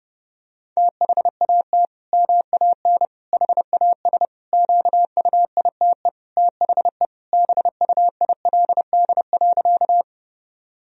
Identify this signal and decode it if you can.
Morse code, heard as a keyed tone, about 32 wpm, 700 Hz. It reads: THAT MAN HAS QUITE THE BUILD.
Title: Sentences from Top 1000 Words - Review Part 1 - 30wpm